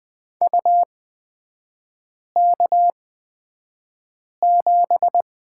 Morse code, heard as a keyed tone, about 20 wpm, spaced as for 5 wpm, 700 Hz. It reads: UK7
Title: 236 Lesson 38e - Letter q - Three Characters 20wpm